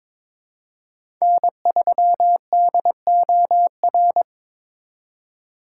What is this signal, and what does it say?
Morse code, heard as a keyed tone, about 21 wpm, 700 Hz. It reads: N3DOR
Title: USA Call Signs - 5 Characters - 2x Character Spacing - 22wpm